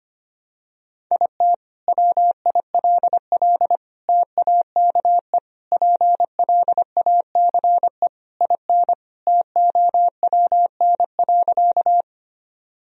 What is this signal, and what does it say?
Morse code, heard as a keyed tone, about 25 wpm, 700 Hz. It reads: IT WILL TAKE PLACE IN TOWN.
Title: Sentences from Top 1000 Words - Review Part 2 - 25wpm